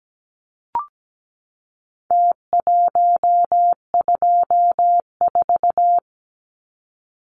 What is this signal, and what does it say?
Morse code, heard as a keyed tone, about 17 wpm, 700 Hz. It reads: T124
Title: Numbers - Contest Style - 4-Digit 17wpm